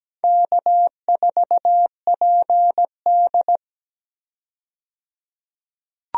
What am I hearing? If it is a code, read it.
Morse code, heard as a keyed tone, about 17 wpm, 700 Hz. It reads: K4PD
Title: USA Call Signs - 4 Characters - 17wpm